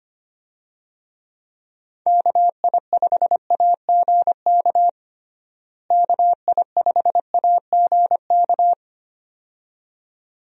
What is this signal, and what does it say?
Morse code, heard as a keyed tone, about 25 wpm, 700 Hz. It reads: KI5AGK KI5AGK
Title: USA Call Signs - 6 Characters - Repeated 2x - 25wpm